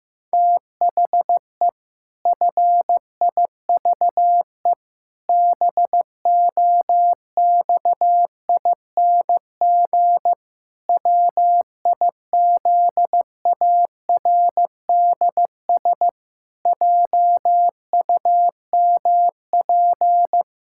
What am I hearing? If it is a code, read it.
Morse code, heard as a keyed tone, about 16 wpm, 700 Hz. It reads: THE FIVE BOXING WIZARDS JUMP
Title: Pangram Challenge v2 - 15wpm